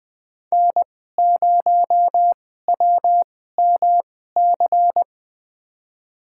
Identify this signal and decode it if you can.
Morse code, heard as a keyed tone, about 21 wpm, spaced as for 15 wpm, 700 Hz. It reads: N0WMC
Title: USA Call Signs - 5 Characters - 2x Character Spacing - 20wpm